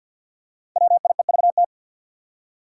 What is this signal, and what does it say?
Morse code, heard as a keyed tone, about 51 wpm, 700 Hz. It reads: WIEVT
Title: Five Challenging Letter-Number Combinations 50wpm